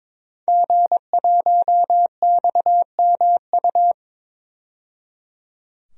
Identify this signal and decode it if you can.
Morse code, heard as a keyed tone, about 22 wpm, 700 Hz. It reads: G1XMU
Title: DXCC-100 Call Signs - 5 Characters - 22wpm